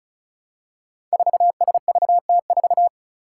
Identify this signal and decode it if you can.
Morse code, heard as a keyed tone, about 34 wpm, 700 Hz. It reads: 4SVT4
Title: Five Challenging Letter-Number Combinations 35wpm